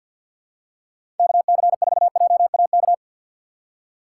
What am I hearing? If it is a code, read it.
Morse code, heard as a keyed tone, about 53 wpm, 700 Hz. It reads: KC4JAK
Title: USA Call Signs - 6 Characters - 6x Character Spacing - 50wpm